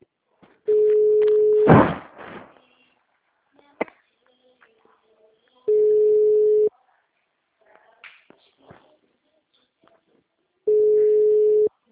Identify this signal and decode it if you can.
Ringback tone (Europe, 425 Hz).